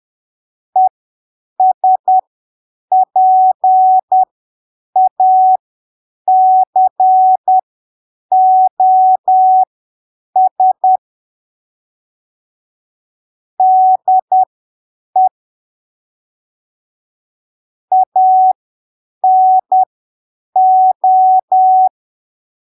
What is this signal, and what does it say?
Morse code, heard as a keyed tone, about 10 wpm, spaced as for 7 wpm, 750 Hz. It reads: ESPACOS DE ANO
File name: arquivos / texto / texto 27 10ppm